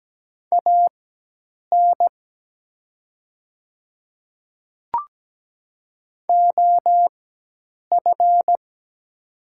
Morse code, heard as a keyed tone, about 17 wpm, spaced as for 8 wpm, 700 Hz. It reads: AN OF
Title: 4x Character Spacing - Top 100 Words - 17wpm